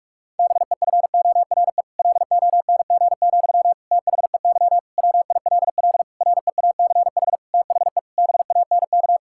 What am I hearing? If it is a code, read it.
Morse code, heard as a keyed tone, about 46 wpm, 700 Hz. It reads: BEFORE LONG, THEY WILL REACH THE BANK
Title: Sentences from Top 1000 Words - Review Part 6 - 45wpm